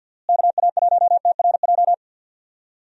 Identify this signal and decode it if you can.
Morse code, heard as a keyed tone, about 52 wpm, 700 Hz. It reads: KA1TRJ
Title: USA Call Signs - 6 Characters - Repeated 2x - 50wpm